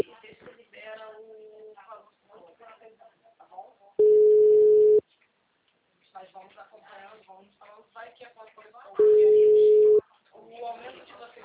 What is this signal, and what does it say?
Ringback tone (Europe, 425 Hz), with speech.
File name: out- -2037- - - .33332